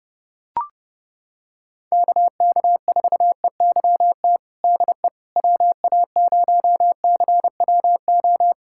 Morse code, heard as a keyed tone, about 30 wpm, 700 Hz. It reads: KK4EYT DE WA0CWO
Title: USA Callsign DE Callsign - 6 Characters - 30wpm